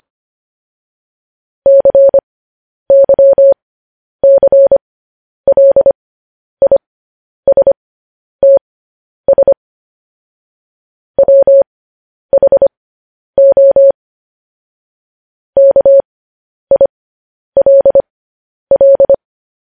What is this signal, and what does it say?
Morse code, heard as a keyed tone, about 25 wpm, spaced as for 10 wpm, 550 Hz. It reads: CYCLISTS WHO KILL
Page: news headlines in morse code at 10 wpm